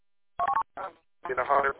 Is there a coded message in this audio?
No, only speech.